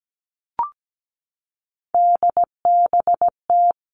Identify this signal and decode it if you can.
Morse code, heard as a keyed tone, about 17 wpm, 700 Hz. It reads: DBT
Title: Three Challenging Letter-Number Combinations 17wpm